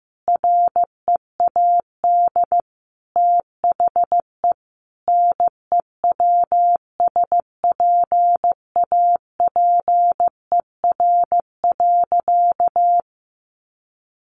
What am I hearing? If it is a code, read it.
Morse code, heard as a keyed tone, about 15 wpm, 700 Hz. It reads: READ THE NEWSPAPER.
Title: Sentences from Top 1400 Words 15wpm